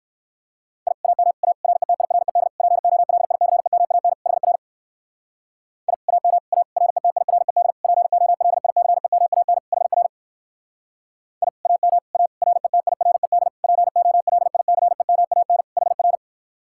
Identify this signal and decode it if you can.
Morse code, heard as a keyed tone, about 59 wpm, 700 Hz. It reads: I AM A RETIRED POLICEMAN HR I AM A RETIRED POLICEMAN HR I AM A RETIRED POLICEMAN HR